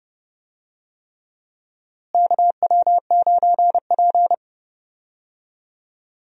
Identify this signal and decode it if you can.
Morse code, heard as a keyed tone, about 30 wpm, 700 Hz. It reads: KW9P
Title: USA Call Signs - 4 Characters - Repeated 3x - 30wpm